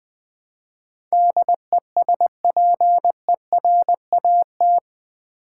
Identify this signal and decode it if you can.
Morse code, heard as a keyed tone, about 20 wpm, 700 Hz. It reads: DESPERAT
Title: Build A Word - 4 Letters At A Time - 11-Letter Words - 20wpm